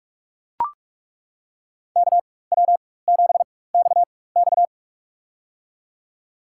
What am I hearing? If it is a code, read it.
Morse code, heard as a keyed tone, about 43 wpm, spaced as for 21 wpm, 700 Hz. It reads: KW7XX
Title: USA Call Signs - 5 Characters - 4x Character Spacing - 45wpm